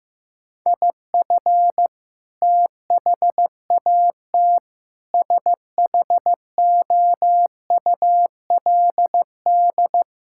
Morse code, heard as a keyed tone, about 15 wpm, 700 Hz. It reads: IF THAT SHOULD
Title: Sentences from Top 1000 Words - Review Part 6 - 15wpm